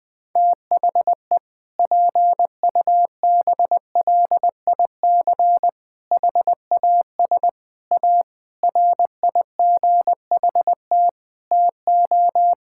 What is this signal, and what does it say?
Morse code, heard as a keyed tone, about 20 wpm, 700 Hz. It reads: THE PUBLIC HAS A RIGHT TO
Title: Sentences with 7-Word Phrases - 20wpm